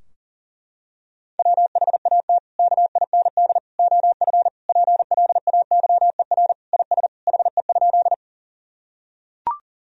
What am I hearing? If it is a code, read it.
Morse code, heard as a keyed tone, about 40 wpm, 700 Hz. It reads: WHAT KIND OF PLAYER IS HE?
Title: Sentences from Top 1000 Words - Review Part 4 - 40wpm